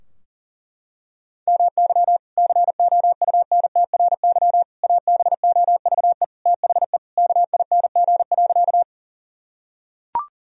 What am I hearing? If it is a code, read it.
Morse code, heard as a keyed tone, about 40 wpm, 700 Hz. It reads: MY COUNTRY ABOVE THE KING.
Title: Sentences from Top 1000 Words - Review Part 2 - 40wpm